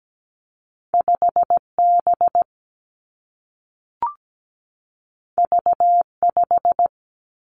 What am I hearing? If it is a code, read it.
Morse code, heard as a keyed tone, about 17 wpm, 700 Hz. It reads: 5B V5
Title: Two Challenging Letter-Number Combinations 17wpm